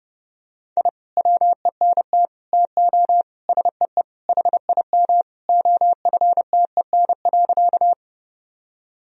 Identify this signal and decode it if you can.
Morse code, heard as a keyed tone, about 31 wpm, 700 Hz. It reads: I WENT TO SEE HIM OFTEN.
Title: Sentences from Top 1000 Words - Review Part 1 - 30wpm